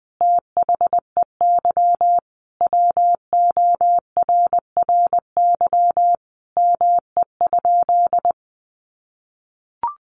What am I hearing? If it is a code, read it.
Morse code, heard as a keyed tone, about 20 wpm, 700 Hz. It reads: THEY WORRY ME?
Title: Sentences from Top 1100 Words 20wpm